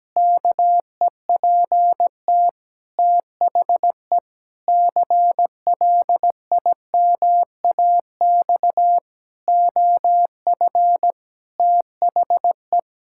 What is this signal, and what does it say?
Morse code, heard as a keyed tone, about 17 wpm, 700 Hz. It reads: KEPT THE CLIMAX OF THE
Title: Pangram Challenge v2 - 17wpm